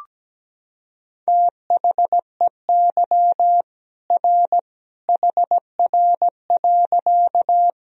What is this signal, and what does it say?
Morse code, heard as a keyed tone, about 17 wpm, 700 Hz. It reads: THEY R HR.
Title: Sentences from Top 200 Words and Abbreviations 17wpm